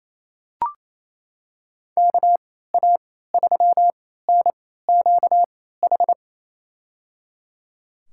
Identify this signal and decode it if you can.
Morse code, heard as a keyed tone, about 28 wpm, spaced as for 16 wpm, 700 Hz. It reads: KA3NQH
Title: USA Call Signs - 6 Characters - 3x Character Spacing - 28wpm